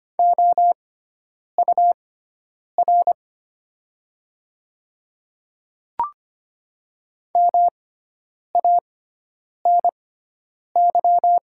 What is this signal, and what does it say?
Morse code, heard as a keyed tone, about 25 wpm, spaced as for 9 wpm, 700 Hz. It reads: OUR MANY
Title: 6x Character Spacing - Top 200 Words - 25wpm